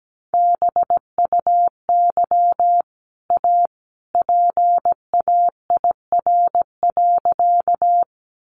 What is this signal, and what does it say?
Morse code, heard as a keyed tone, about 17 wpm, 700 Hz. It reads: BUY A PAIR.
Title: Sentences from Top 1200 Words 17wpm